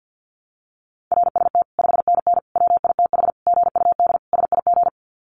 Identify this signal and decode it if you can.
Morse code, heard as a keyed tone, about 50 wpm, 700 Hz. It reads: RST 5NN WITH GUD SIG
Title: Common QSO Phrases 50wpm